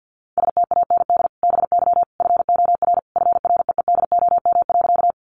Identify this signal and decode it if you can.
Morse code, heard as a keyed tone, about 50 wpm, 700 Hz. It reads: STAND BY FOR FREEDOM.
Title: Sentences from Top 800 Words 50wpm